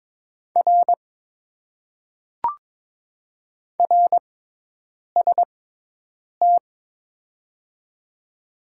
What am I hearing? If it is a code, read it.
Morse code, heard as a keyed tone, about 22 wpm, 700 Hz. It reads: R R S T
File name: Top 90 QSO Elements - 6x Character Spacing - 22wpm